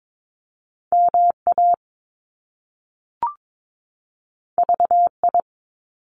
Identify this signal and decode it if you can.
Morse code, heard as a keyed tone, about 22 wpm, 700 Hz. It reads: MA VI
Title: Top Two Letter Combinations in Words 22wpm